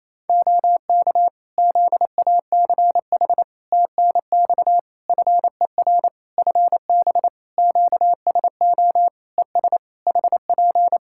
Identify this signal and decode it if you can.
Morse code, heard as a keyed tone, about 27 wpm, 700 Hz. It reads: OK ZACH TNX FER FB QSO ES HP
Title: K1USN SST QSO Practice 28wpm